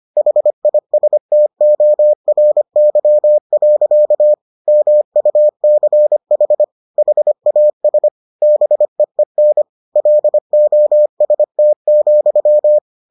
Morse code, heard as a keyed tone, about 25 wpm, 600 Hz. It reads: HISTORY. MUCH HAS BEEN LOST,